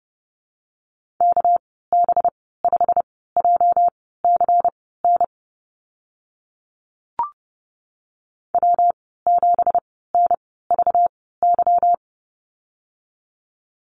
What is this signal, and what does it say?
Morse code, heard as a keyed tone, about 30 wpm, spaced as for 17 wpm, 700 Hz. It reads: KB5JCN W7NVY
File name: Lesson 37g - Letter x - Call Signs 30wpm